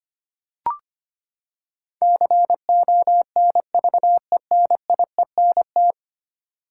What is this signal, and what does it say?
Morse code, heard as a keyed tone, about 25 wpm, 700 Hz. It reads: CONVENIENT
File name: Ten Letter Words - 25wpm